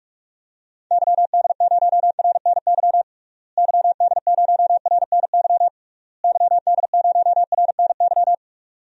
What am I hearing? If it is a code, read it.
Morse code, heard as a keyed tone, about 42 wpm, 700 Hz. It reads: YD0RNY YD0RNY YD0RNY
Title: DXCC-100 Call Signs - 6 Characters - Repeated 3x - 45wpm